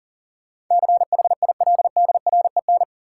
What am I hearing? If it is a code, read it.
Morse code, heard as a keyed tone, about 40 wpm, 700 Hz. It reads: CHILDREN
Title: Sentences from Top 300 Words - New Word Drill - 40wpm